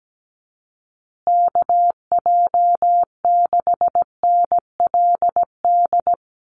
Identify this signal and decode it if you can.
Morse code, heard as a keyed tone, about 17 wpm, 700 Hz. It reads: KJ6NLD
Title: USA Call Signs Repeated 2x 17wpm